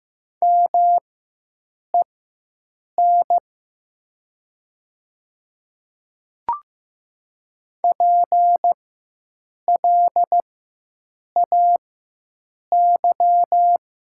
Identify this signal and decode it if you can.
Morse code, heard as a keyed tone, about 15 wpm, spaced as for 7 wpm, 700 Hz. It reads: MEN PLAY
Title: 4x Character Spacing - Top 200 Words - 15wpm